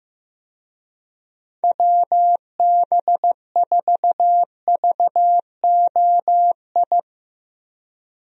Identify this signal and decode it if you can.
Morse code, heard as a keyed tone, about 15 wpm, 700 Hz. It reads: WB4VOI
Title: USA Call Signs - 6 Characters - Repeated 3x - 15wpm